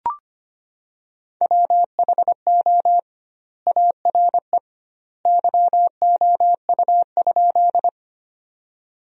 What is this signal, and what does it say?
Morse code, heard as a keyed tone, about 25 wpm, 700 Hz. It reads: WHO ARE YOU?
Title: 2x Word-Spacing - Sentences from Top 100 Words - 25wpm